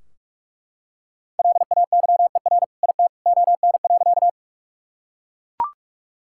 Morse code, heard as a keyed tone, about 41 wpm, 700 Hz. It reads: LAYER IT ON.
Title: Sentences from Top 1600 Words 45wpm